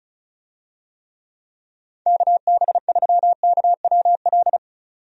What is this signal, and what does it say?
Morse code, heard as a keyed tone, about 36 wpm, 700 Hz. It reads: KB3KWL
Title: USA Call Signs - 6 Characters - 35wpm